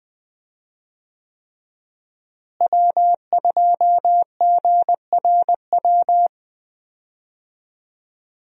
Morse code, heard as a keyed tone, about 20 wpm, 700 Hz. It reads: W2GRW
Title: USA Call Signs - 5 Characters - Repeated 3x - 20wpm